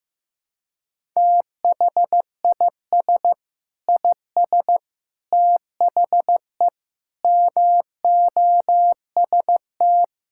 Morse code, heard as a keyed tone, about 15 wpm, 700 Hz. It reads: THIS IS THE MOST
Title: Sentences from Top 1000 Words - Review Part 6 - 15wpm